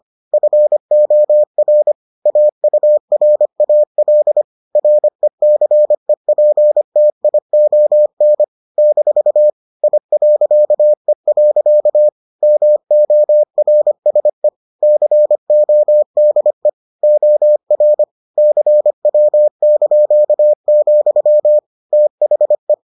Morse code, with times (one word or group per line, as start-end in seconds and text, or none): 0.3-1.9 FOR
2.3-4.4 AURAL
4.8-8.4 RECEPTION
8.8-9.5 -
9.8-12.1 I.E.
12.4-14.5 MORSE
14.8-16.7 CODE
17.0-18.0 OR
18.4-21.6 CW),
21.9-22.7 THE